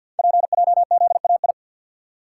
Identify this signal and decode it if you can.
Morse code, heard as a keyed tone, about 50 wpm, 690 Hz. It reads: PJZAI
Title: Code Groups - 5 Letters - 50wpm